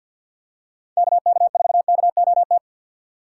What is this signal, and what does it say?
Morse code, heard as a keyed tone, about 50 wpm, 700 Hz. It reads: KK4KOT
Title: USA Call Signs - 2x Character Spacing - 50wpm